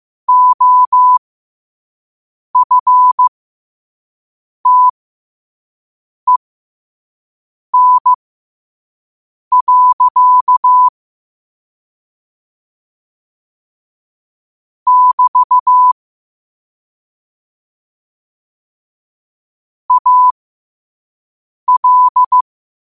Morse code, heard as a keyed tone, about 15 wpm, spaced as for 5 wpm, 1000 Hz. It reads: OFTEN. <BT> AL
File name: qotdfiles / 5wpm